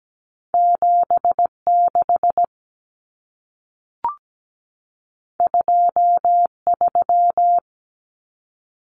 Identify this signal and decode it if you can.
Morse code, heard as a keyed tone, about 17 wpm, 700 Hz. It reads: 76 23
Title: Common Cut Numbers - 2-Digit 17wpm